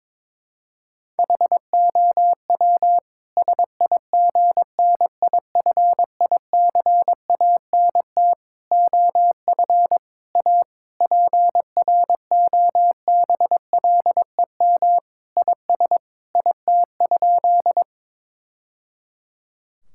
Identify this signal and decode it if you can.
Morse code, heard as a keyed tone, about 21 wpm, 700 Hz. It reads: HOW SIGNIFICANT OF A PROBLEM IS IT?
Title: Sentences from Top 1000 Words - Review Part 6 - 22wpm